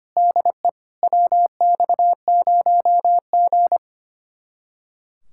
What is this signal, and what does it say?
Morse code, heard as a keyed tone, about 29 wpm, 700 Hz. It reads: DE WX0G
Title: Speed-Racing Common QSO Phrases - 25wpm